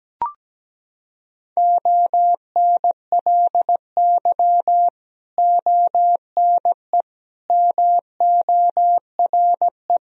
Morse code, heard as a keyed tone, about 17 wpm, 700 Hz. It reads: ONLY ONE MORE
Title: Sentences from Top 1000 Words - Review Part 2 - 17wpm